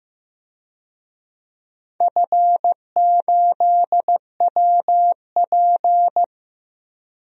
Morse code, heard as a keyed tone, about 15 wpm, 700 Hz. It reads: F8WP
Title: DXCC-100 Call Signs - 4 Characters - 15wpm